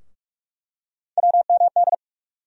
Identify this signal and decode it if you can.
Morse code, heard as a keyed tone, about 47 wpm, 700 Hz. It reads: WMD
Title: USA Call Sign Postfixes - 3 Letters 45wpm